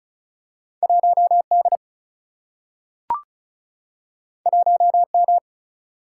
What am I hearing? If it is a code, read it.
Morse code, heard as a keyed tone, about 36 wpm, 700 Hz. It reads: 1D 1M
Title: Single-Letter Single-Number 35wpm